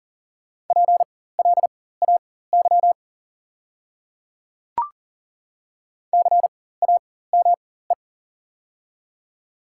Morse code, heard as a keyed tone, about 39 wpm, spaced as for 19 wpm, 690 Hz. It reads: PLAY CAME
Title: 4x Character Spacing - Top 300 Words - 40wpm